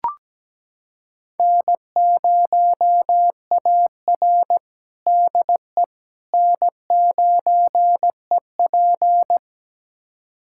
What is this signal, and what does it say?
Morse code, heard as a keyed tone, about 17 wpm, 700 Hz. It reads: N0AR DE N9EP
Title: USA Callsign DE Callsign - 4 Characters - 17wpm